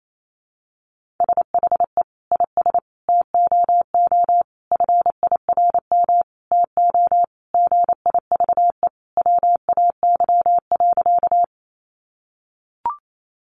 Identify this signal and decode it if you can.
Morse code, heard as a keyed tone, about 28 wpm, 700 Hz. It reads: SHE IS TOO FIRM TO GIVE WAY.
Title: Sentences from Top 1400 Words 28wpm